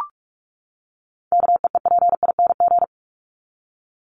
Morse code, heard as a keyed tone, about 44 wpm, 700 Hz. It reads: KEEPING
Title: ING Suffix - Top 100 Words with Base Words 45wpm